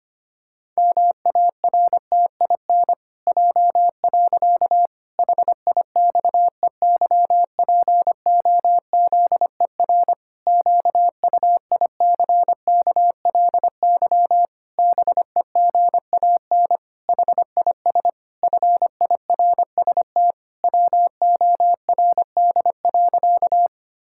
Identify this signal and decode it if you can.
Morse code, heard as a keyed tone, about 25 wpm, 700 Hz. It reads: MARTIN J. HIXEYPOZER QUICKLY BEGAN HIS FIRST WORD.
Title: Pangram Challenge v2 - 25wpm